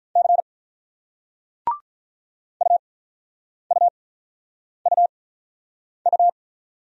Morse code, heard as a keyed tone, about 44 wpm, spaced as for 9 wpm, 700 Hz. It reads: C UUUU
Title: Speed-Racing Single Letter-Number 35wpm